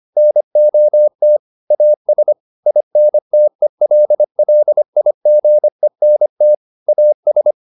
Morse code, heard as a keyed tone, about 25 wpm, 600 Hz. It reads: NOT AS INTELLIGENT AS
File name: the old man and the sea part 1 25wpm